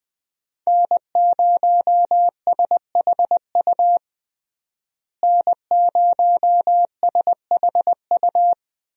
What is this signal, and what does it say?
Morse code, heard as a keyed tone, about 20 wpm, 700 Hz. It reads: N0SHU N0SHU
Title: USA Call Signs - 5 Characters - Repeated 2x - 20wpm